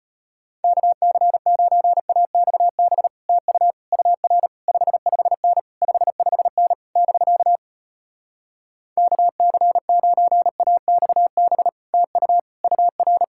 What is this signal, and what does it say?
Morse code, heard as a keyed tone, about 33 wpm, 700 Hz. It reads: KC9AXB TU UR 55N 55N <BK> KC9AXB TU UR